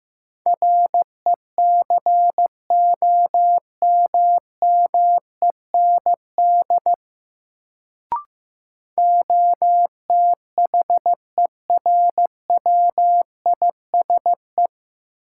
Morse code, heard as a keyed tone, about 15 wpm, 700 Hz. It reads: RECOMMEND OTHERWISE
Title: Sentences from Top 1200 Words - New Word Drill Alt - 15wpm